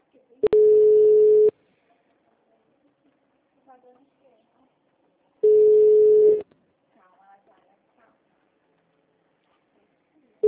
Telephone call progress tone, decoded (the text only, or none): ringback tone